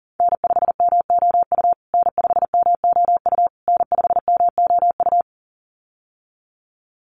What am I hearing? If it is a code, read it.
Morse code, heard as a keyed tone, about 40 wpm, 700 Hz. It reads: N5MOU N5MOU N5MOU
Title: USA Call Signs Repeated 3x 40wpm